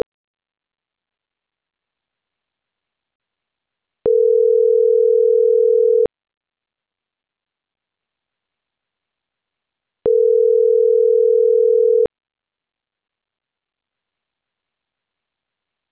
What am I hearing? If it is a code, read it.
Ringback tone (North America), 440 + 480 Hz.